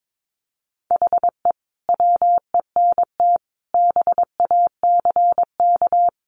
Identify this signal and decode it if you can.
Morse code, heard as a keyed tone, about 22 wpm, 700 Hz. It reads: HE WENT BACK
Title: Sentences from Top 1100 Words 22wpm